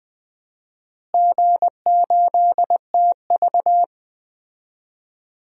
Morse code, heard as a keyed tone, about 20 wpm, 700 Hz. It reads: G8TV
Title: DXCC-100 Call Signs - 4 Characters - Repeated 2x - 20wpm